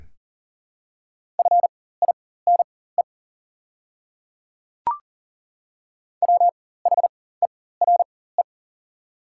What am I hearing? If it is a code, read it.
Morse code, heard as a keyed tone, about 40 wpm, spaced as for 19 wpm, 700 Hz. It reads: FINE WHERE
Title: 4x Character Spacing - Top 500 Words - 40wpm